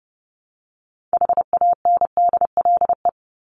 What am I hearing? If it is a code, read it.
Morse code, heard as a keyed tone, about 30 wpm, 700 Hz. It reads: HANDLE
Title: Top 1001-1250 Words 30wpm